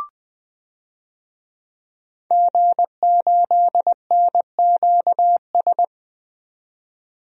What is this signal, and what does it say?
Morse code, heard as a keyed tone, about 20 wpm, 700 Hz. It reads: G8NQS
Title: DXCC-100 Call Signs - 5 Characters - 20wpm